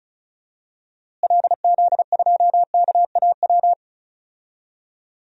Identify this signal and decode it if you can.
Morse code, heard as a keyed tone, about 36 wpm, 700 Hz. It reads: LZ2KAW
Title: DXCC-100 Call Signs - 6 Characters - Repeated 2x - 35wpm